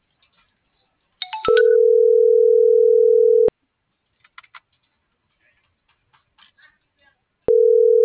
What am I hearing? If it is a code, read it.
Ringback tone (North America), 440 + 480 Hz.